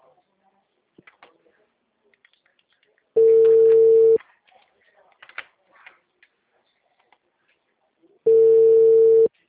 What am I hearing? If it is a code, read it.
Ringback tone (China, 450 Hz), with speech.
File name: out- -2007